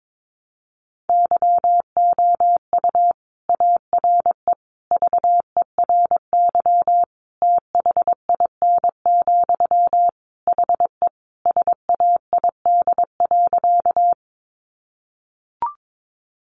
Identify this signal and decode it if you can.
Morse code, heard as a keyed tone, about 22 wpm, 700 Hz. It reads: YOU ARE VERY THIN, HE SAID.
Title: Sentences from Top 1200 Words 22wpm